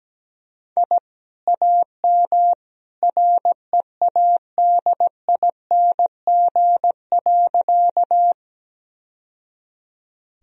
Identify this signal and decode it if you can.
Morse code, heard as a keyed tone, about 17 wpm, 700 Hz. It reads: I AM READING.
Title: Sentences from Top 1000 Words - Review Part 3 - 17wpm